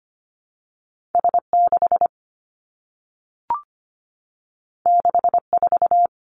Morse code, heard as a keyed tone, about 25 wpm, 700 Hz. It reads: S 6 6 4
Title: Two Challenging Letter-Number Combinations 25wpm